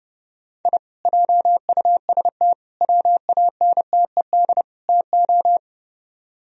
Morse code, heard as a keyed tone, about 30 wpm, 700 Hz. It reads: I JUST WANTED TO